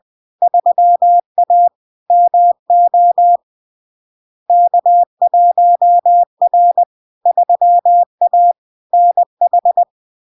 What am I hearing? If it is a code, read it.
Morse code, heard as a keyed tone, about 20 wpm, 700 Hz. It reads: 3A MO K1R 3A NH
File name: LARK-FD-2022-Log-in-CW-20wpm-700-Hz